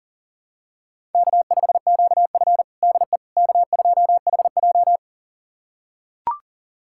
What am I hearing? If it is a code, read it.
Morse code, heard as a keyed tone, about 40 wpm, 700 Hz. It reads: K5QF DE K2HJ
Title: USA Callsign DE Callsign - 4 Characters - 40wpm